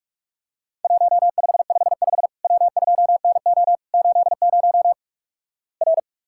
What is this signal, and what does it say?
Morse code, heard as a keyed tone, about 46 wpm, 700 Hz. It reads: 1555 W2NO 80 R
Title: Worked All Europe DX Contest - QTC Exchange - 45wpm